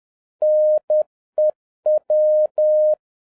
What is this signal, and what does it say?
Morse code, heard as a keyed tone, about 10 wpm, 610 Hz. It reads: NEW